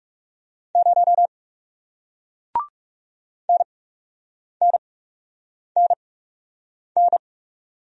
Morse code, heard as a keyed tone, about 41 wpm, spaced as for 8 wpm, 700 Hz. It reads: 0 NNNN